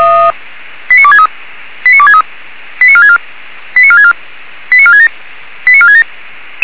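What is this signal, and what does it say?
Five-tone selective call (ZVEI), heard as quick runs of five tones, tones of about 70 ms. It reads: 89263 89263 89464 89464 89467 89467
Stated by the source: Five-tone ZVEI selective call received on SDR.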